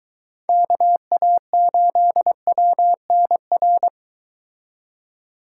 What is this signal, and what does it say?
Morse code, heard as a keyed tone, about 23 wpm, 700 Hz. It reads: KA8WNR